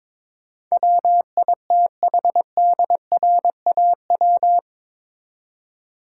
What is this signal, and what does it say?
Morse code, heard as a keyed tone, about 22 wpm, 700 Hz. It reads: WITHDRAW
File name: practice / 22wpm / Eight Letter Words - 22wpm